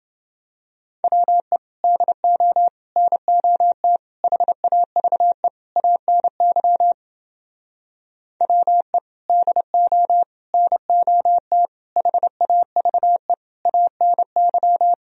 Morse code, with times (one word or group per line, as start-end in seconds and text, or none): 1.0-1.6 WE
1.8-2.7 DO
3.0-4.0 NOT
4.2-5.5 HAVE
5.8-6.9 ANY
8.4-9.0 WE
9.3-10.2 DO
10.5-11.7 NOT
12.0-13.3 HAVE
13.7-14.9 ANY